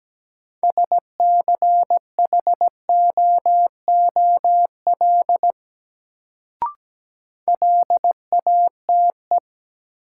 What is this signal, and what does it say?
Morse code, heard as a keyed tone, about 17 wpm, 700 Hz. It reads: SCHOOL LATE
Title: Sentences from Top 300 Words - New Word Drill - 17wpm